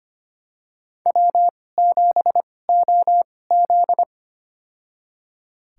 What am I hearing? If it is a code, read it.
Morse code, heard as a keyed tone, about 25 wpm, spaced as for 18 wpm, 700 Hz. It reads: W7OZ
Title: USA Call Signs - 2x Character Spacing - 25wpm